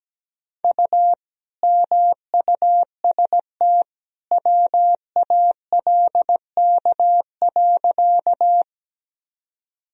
Morse code, heard as a keyed tone, about 17 wpm, 700 Hz. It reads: U MUST WALK.
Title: Sentences from Top 300 Words and Abbreviations 17wpm